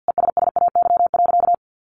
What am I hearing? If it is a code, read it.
Morse code, heard as a keyed tone, about 50 wpm, 700 Hz. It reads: ESSAY.
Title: Sentences from Top 1500 Words 50wpm